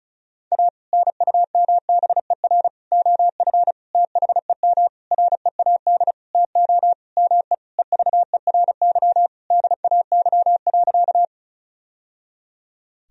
Morse code, heard as a keyed tone, about 36 wpm, 700 Hz. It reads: A NUMBER OF THEM READ TO ME EVERY DAY.
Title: Sentences from Top 1000 Words - Review Part 1 - 35wpm